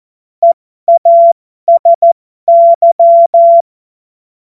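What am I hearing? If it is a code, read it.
Morse code, heard as a keyed tone, about 14 wpm, spaced as for 12 wpm, 680 Hz. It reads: EASY
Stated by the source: Morse joke practice at effective 60, Farnsworth 70.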